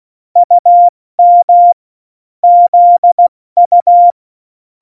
Morse code, heard as a keyed tone, about 16 wpm, 700 Hz. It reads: UM ZU
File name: Morsekurs / Witze / Witz 13. Oktober 2020, 70 F80